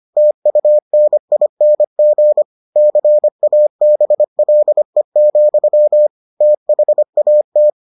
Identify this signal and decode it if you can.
Morse code, heard as a keyed tone, about 25 wpm, 600 Hz. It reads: TUNING CABLE, THAT